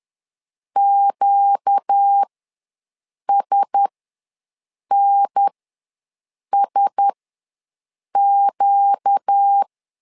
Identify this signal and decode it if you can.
Morse code, heard as a keyed tone, about 10 wpm, spaced as for 6 wpm, 800 Hz. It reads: QSNSQ
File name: LEZIONI / 4 Q1S ws 10wpm 2208